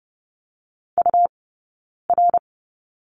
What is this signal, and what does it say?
Morse code, heard as a keyed tone, about 30 wpm, 700 Hz. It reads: U R
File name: Lesson 17d - Letter u - Two Characters 30wpm